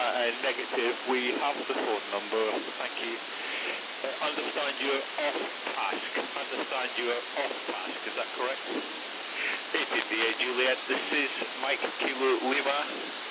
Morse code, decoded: none

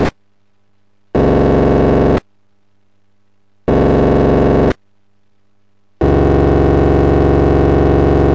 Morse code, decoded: none

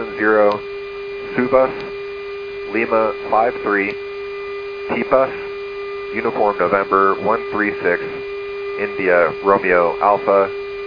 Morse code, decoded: none

